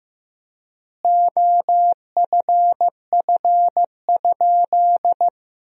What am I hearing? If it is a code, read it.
Morse code, heard as a keyed tone, about 15 wpm, 700 Hz. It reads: OFF?
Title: 3x Word-Spacing - Sentences from Top 500 Words - 15wpm